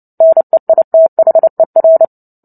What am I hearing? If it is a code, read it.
Morse code, heard as a keyed tone, about 30 wpm, 650 Hz. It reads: NEITHER
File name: Book Two / 03-Chapter 3 - 29 WPM